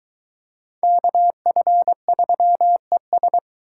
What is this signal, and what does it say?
Morse code, heard as a keyed tone, about 23 wpm, 700 Hz. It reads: KF3ES